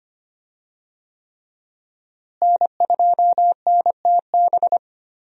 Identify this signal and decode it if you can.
Morse code, heard as a keyed tone, about 25 wpm, 700 Hz. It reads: N2NTB
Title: USA Call Signs - 5 Characters - Repeated 3x - 25wpm